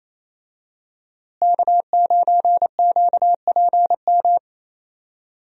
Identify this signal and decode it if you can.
Morse code, heard as a keyed tone, about 28 wpm, 700 Hz. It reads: K9QPM
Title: USA Call Signs - 5 Characters - Repeated 2x - 28wpm